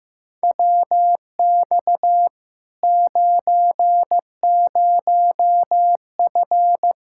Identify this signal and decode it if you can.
Morse code, heard as a keyed tone, about 15 wpm, 700 Hz. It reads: WX 90F
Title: Practice QSO - Exchange 2 - Info - 15wpm